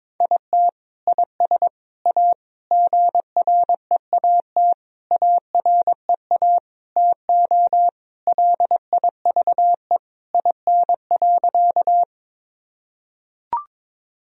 Morse code, heard as a keyed tone, about 22 wpm, 700 Hz. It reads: IT IS A GREAT AREA TO LIVE IN.
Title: Sentences from Top 1000 Words - Review Part 2 - 22wpm